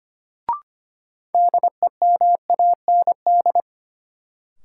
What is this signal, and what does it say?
Morse code, heard as a keyed tone, about 25 wpm, 700 Hz. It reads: DEMAND
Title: Sentences from Top 800 Words - New Word Drill - 25wpm